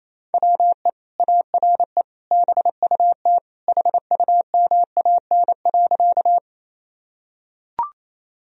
Morse code, heard as a keyed tone, about 28 wpm, 700 Hz. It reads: WE ARE BUT HUMAN.
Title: Sentences from Top 1000 Words - Review Part 3 - 28wpm